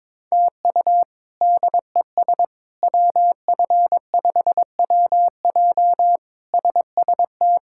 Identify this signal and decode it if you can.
Morse code, heard as a keyed tone, about 21 wpm, 700 Hz. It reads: TU DES WF5WJ SST
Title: K1USN SST QSO Practice - 22wpm